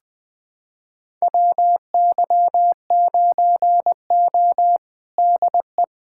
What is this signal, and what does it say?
Morse code, heard as a keyed tone, about 20 wpm, 700 Hz. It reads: WY9O DE
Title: USA Callsign DE Callsign - 4 Characters - 20wpm